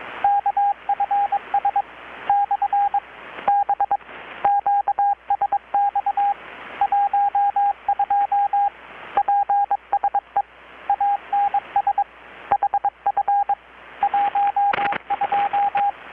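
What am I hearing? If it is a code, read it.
Morse code, heard as a keyed tone, about 21 wpm, 810 Hz, with radio noise. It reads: KFS / B QSX 12 PSE ANS HF 12
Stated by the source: Morse from KPH's Night of Nights XX maritime radiotelegraphy event, received in Chiapas, Mexico about 3,800 km away on a Kenwood R-600 with a grounded 10 m long-wire antenna.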